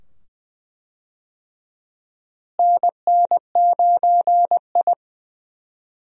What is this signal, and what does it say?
Morse code, heard as a keyed tone, about 20 wpm, 700 Hz. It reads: NN9I
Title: USA Call Signs - 4 Characters - Repeated 2x - 20wpm